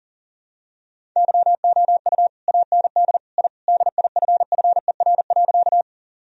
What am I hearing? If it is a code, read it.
Morse code, heard as a keyed tone, about 40 wpm, 700 Hz. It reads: YOU AND I DIFFER.
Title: Sentences from Top 1000 Words - Review Part 1 - 40wpm